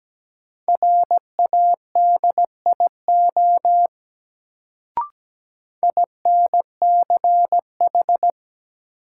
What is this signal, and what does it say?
Morse code, heard as a keyed tone, about 17 wpm, 700 Hz. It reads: RADIO INCH
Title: Sentences from Top 1000 Words - Review Part 3 - New Word Drill - 17wpm